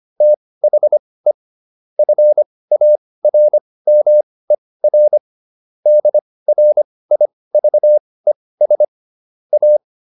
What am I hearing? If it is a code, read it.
Morse code, heard as a keyed tone, about 25 wpm, spaced as for 18 wpm, 600 Hz. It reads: THE FARMER DRIVES A